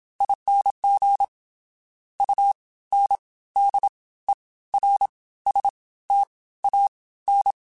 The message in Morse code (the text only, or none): ING UNDERSTAN